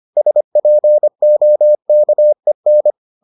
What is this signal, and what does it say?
Morse code, heard as a keyed tone, about 25 wpm, 600 Hz. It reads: SPOKEN